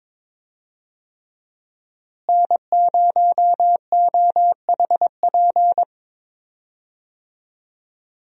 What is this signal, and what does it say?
Morse code, heard as a keyed tone, about 22 wpm, 700 Hz. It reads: N0OHP
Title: USA Call Signs - 5 Characters - Repeated 3x - 22wpm